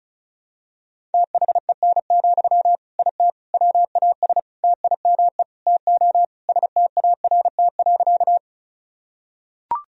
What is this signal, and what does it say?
Morse code, heard as a keyed tone, about 34 wpm, 700 Hz. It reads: THEN, IT WAS TIME TO START.
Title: Sentences from Top 1000 Words - Review Part 1 - 35wpm